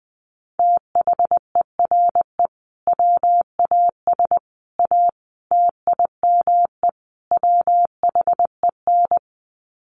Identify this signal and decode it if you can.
Morse code, heard as a keyed tone, about 20 wpm, 700 Hz. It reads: THERE WAS A TIME WHEN
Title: Sets of 5 Words 20wpm